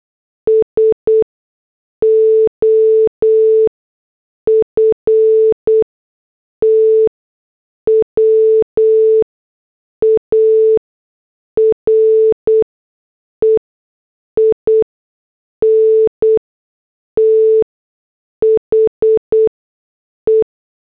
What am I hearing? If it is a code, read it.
Morse code, heard as a keyed tone, about 8 wpm, 440 Hz. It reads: S O F T W A R E I N T H E
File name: morse / headlines